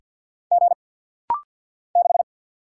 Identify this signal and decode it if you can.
Morse code, heard as a keyed tone, about 50 wpm, 700 Hz. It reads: G 6
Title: Single Letter-Number - Rapid-Fire 50wpm